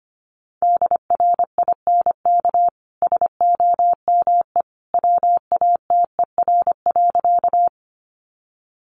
Morse code, heard as a keyed tone, about 25 wpm, 700 Hz. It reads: DRINK SOME WATER.
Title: Sentences from Top 1500 Words 25wpm